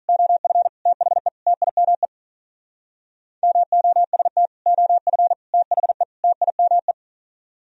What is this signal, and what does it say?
Morse code, heard as a keyed tone, about 45 wpm, 700 Hz. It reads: OF THE TIME MOST OF THE TIME